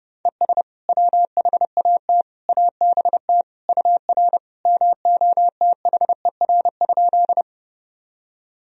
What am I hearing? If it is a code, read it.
Morse code, heard as a keyed tone, about 30 wpm, 700 Hz. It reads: ES WHAT ABT UR MOTHER?
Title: Sentences from Top 200 Words and Abbreviations 30wpm